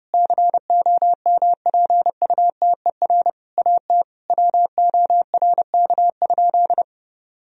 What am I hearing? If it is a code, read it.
Morse code, heard as a keyed tone, about 30 wpm, 700 Hz. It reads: COMPUTER AT WORK?
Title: Sentences from Top 1000 Words - Review Part 3 - 30wpm